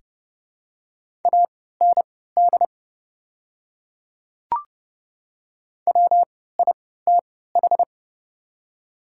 Morse code, heard as a keyed tone, about 31 wpm, spaced as for 17 wpm, 690 Hz. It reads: AND WITH
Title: 3x Character Spacing - Top 200 Words - 30wpm